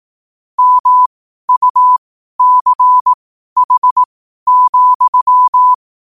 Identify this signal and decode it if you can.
Morse code, heard as a keyed tone, about 18 wpm, spaced as for 12 wpm, 1000 Hz. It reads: MUCH,
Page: Quotes for Thu, 14 Aug 2025 in Morse Code at 12 words per minute.